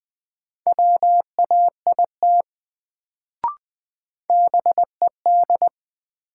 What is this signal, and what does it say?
Morse code, heard as a keyed tone, about 20 wpm, 700 Hz. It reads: WAIT BED
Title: Sentences from Top 500 Words - New Word Drill - 20wpm